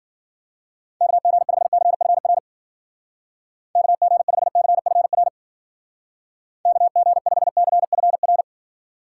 Morse code, heard as a keyed tone, about 51 wpm, 700 Hz. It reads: KG5CFR KG5CFR KG5CFR